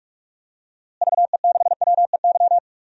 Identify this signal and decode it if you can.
Morse code, heard as a keyed tone, about 46 wpm, 700 Hz. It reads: VE6WEY
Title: DXCC-100 Call Signs - 6 Characters - Repeated 2x - 45wpm